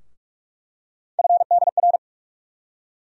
Morse code, heard as a keyed tone, about 46 wpm, 700 Hz. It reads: FDR
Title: USA Call Sign Postfixes - 3 Letters 45wpm